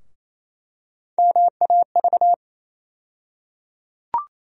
Morse code, heard as a keyed tone, about 28 wpm, 700 Hz. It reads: MAV
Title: USA Call Sign Postfixes - 3 Letters 28wpm